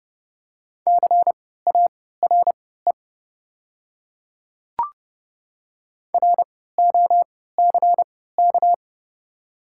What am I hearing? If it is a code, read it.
Morse code, heard as a keyed tone, about 30 wpm, spaced as for 17 wpm, 690 Hz. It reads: CARE ROCK
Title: 3x Character Spacing - Top 500 Words - 30wpm